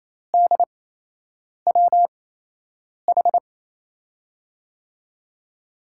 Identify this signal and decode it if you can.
Morse code, heard as a keyed tone, about 28 wpm, spaced as for 8 wpm, 700 Hz. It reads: DWH
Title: USA Call Signs - 5 Characters - 8x Character Spacing - 28wpm